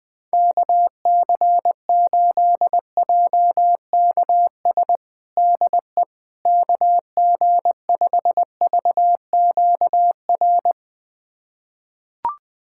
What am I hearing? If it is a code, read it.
Morse code, heard as a keyed tone, about 21 wpm, 700 Hz. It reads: KC8JKS DE KG5VQR
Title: USA Callsign DE Callsign - 6 Characters - 20wpm